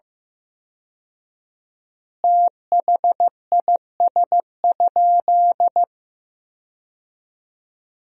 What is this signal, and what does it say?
Morse code, heard as a keyed tone, about 15 wpm, 700 Hz. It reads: THIS?
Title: 4x Word-Spacing - Sentences from Top 100 Words - 15wpm